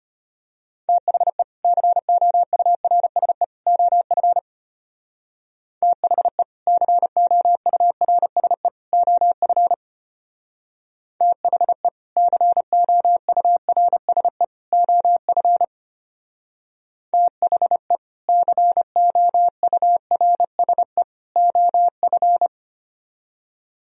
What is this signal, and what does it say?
Morse code, heard as a keyed tone, about 32 wpm, 700 Hz. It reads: THE COURSE OF THE COURSE OF THE COURSE OF THE COURSE OF